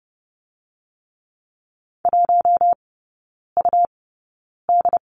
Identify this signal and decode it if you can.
Morse code, heard as a keyed tone, about 30 wpm, 700 Hz. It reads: 1 U D
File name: Lesson 19e - Letter w - Three Characters 30wpm